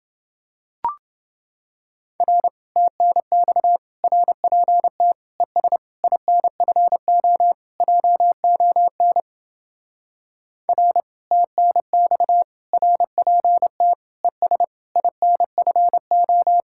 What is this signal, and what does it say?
Morse code, heard as a keyed tone, about 28 wpm, 700 Hz. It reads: R TNX RPT ES INFO JON R TNX RPT ES INFO